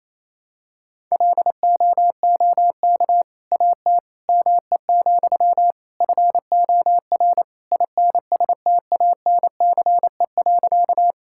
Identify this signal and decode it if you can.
Morse code, heard as a keyed tone, about 28 wpm, 700 Hz. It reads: LOOK AT ME, FOR INSTANCE.
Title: Sentences from Top 1000 Words - Review Part 5 - 28wpm